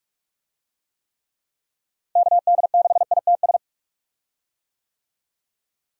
Morse code, heard as a keyed tone, about 46 wpm, 700 Hz. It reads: KD6ITS
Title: USA Call Signs - 6 Characters - Repeated 2x - 45wpm